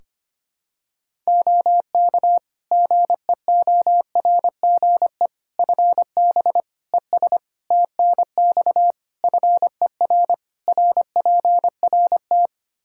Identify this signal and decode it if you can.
Morse code, heard as a keyed tone, about 25 wpm, 700 Hz. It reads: OK GEORGE FB ES TNX FER RPRT
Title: K1USN SST QSO Practice 25wpm